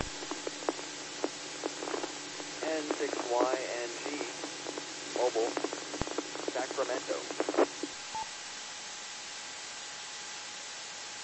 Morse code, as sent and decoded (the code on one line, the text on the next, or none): none